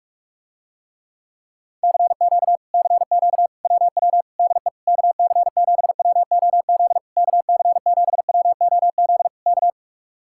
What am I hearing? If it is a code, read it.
Morse code, heard as a keyed tone, about 44 wpm, 700 Hz. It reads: CQ CQ WW DE KC7WOZ KC7WOZ K